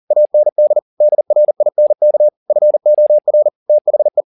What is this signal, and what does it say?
Morse code, heard as a keyed tone, about 40 wpm, 600 Hz. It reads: AND DRINK FOR THE
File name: the old man and the sea 2 40wpm